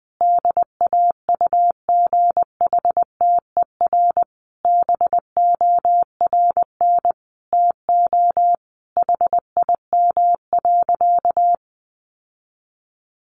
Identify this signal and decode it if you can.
Morse code, heard as a keyed tone, about 20 wpm, 700 Hz. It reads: DAUGHTER BORN TO HIM.
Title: Sentences from Top 1500 Words 20wpm